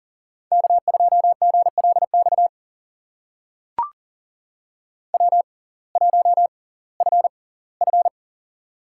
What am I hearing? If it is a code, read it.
Morse code, heard as a keyed tone, about 40 wpm, spaced as for 22 wpm, 700 Hz. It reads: K2GLX W1FF
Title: USA Call Signs - 6x Character Spacing - 40wpm